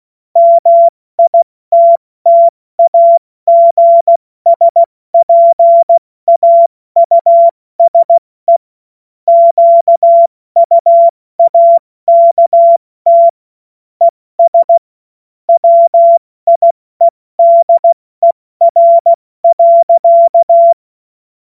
Morse code, heard as a keyed tone, about 16 wpm, 680 Hz. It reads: MITTAGSPAUSE QUAKT ES WIEDER.
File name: Morsekurs / Witze / Witz 05. April 2022 70 F80